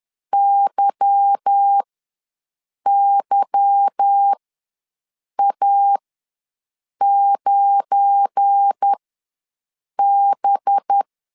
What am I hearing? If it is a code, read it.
Morse code, heard as a keyed tone, about 10 wpm, spaced as for 6 wpm, 800 Hz. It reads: YYA9B
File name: LEZIONI / 8 YJ45 / 8 Y ws 10wpm 3952